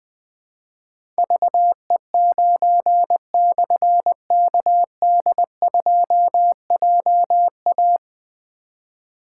Morse code, heard as a keyed tone, about 20 wpm, 700 Hz. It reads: VE9/KD2JA
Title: Difficult Call Signs 20wpm